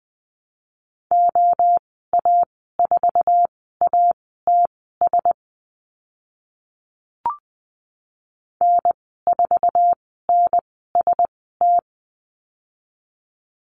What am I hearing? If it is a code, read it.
Morse code, heard as a keyed tone, about 20 wpm, spaced as for 14 wpm, 700 Hz. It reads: OA4ATS N4NST
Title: 062 Lesson 11g - Letter h - Call Signs 20wpm